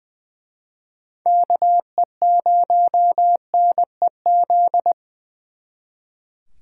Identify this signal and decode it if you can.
Morse code, heard as a keyed tone, about 23 wpm, 700 Hz. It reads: KE0NEZ
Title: Speed Racing - USA 2x3 Call Signs 20wpm